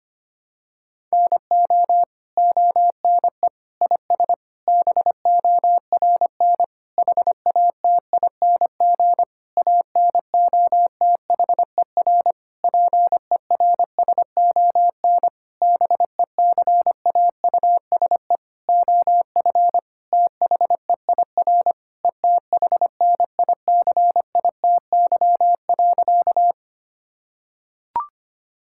Morse code, heard as a keyed tone, about 25 wpm, 700 Hz. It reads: NO ONE IS BORN HATING ANOTHER PERSON BECAUSE OF THEIR ETHNICITY.
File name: practice / 25wpm / Sentences with 9-Word Phrases - 25wpm